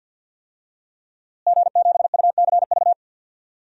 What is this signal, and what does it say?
Morse code, heard as a keyed tone, about 51 wpm, 700 Hz. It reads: G7UCV
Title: DXCC-100 Call Signs - 5 Characters - Repeated 2x - 50wpm